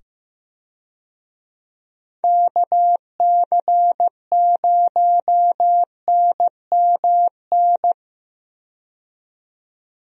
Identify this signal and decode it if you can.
Morse code, heard as a keyed tone, about 15 wpm, 700 Hz. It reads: KC0NMN
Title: USA Call Signs - 6 Characters - 15wpm